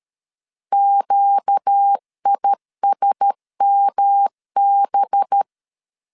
Morse code, heard as a keyed tone, about 13 wpm, 800 Hz. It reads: QISMB